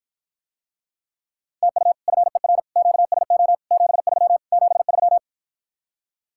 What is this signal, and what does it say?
Morse code, heard as a keyed tone, about 52 wpm, 700 Hz. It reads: TU FER QSO 73 73